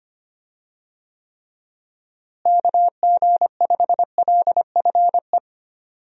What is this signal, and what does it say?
Morse code, heard as a keyed tone, about 25 wpm, 700 Hz. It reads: KG5LFE
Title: USA Call Signs - 6 Characters - Repeated 3x - 25wpm